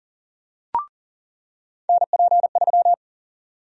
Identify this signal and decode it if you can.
Morse code, heard as a keyed tone, about 40 wpm, 700 Hz. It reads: NP3
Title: USA Call Sign Prefixes - 40wpm